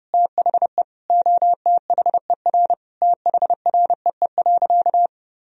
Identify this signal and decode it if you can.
Morse code, heard as a keyed tone, about 30 wpm, 700 Hz. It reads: THE OTHER THREE.
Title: Sentences from Top 1000 Words - Review Part 4 - 30wpm